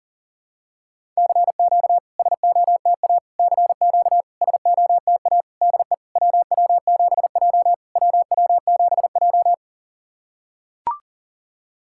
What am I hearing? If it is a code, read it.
Morse code, heard as a keyed tone, about 40 wpm, 690 Hz. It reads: CQ SOTA CQ SOTA DE WW7J WW7J
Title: SOTA QSO Practice 40wpm